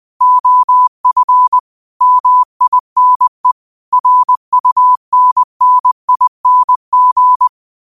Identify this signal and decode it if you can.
Morse code, heard as a keyed tone, about 20 wpm, 1000 Hz. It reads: OF MINE RUNNING